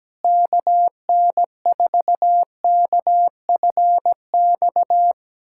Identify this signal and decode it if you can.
Morse code, heard as a keyed tone, about 17 wpm, 700 Hz. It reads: KN4KFX
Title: USA Callsign DE Callsign - 6 Characters - 17wpm